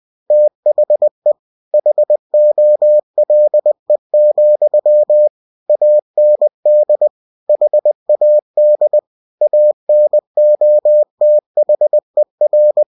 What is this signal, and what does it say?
Morse code, heard as a keyed tone, about 20 wpm, 600 Hz. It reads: THE HOLE, AND HAD ANOTHER
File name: cw-books / winnie-the-pooh / winnie-20wpm-page-0014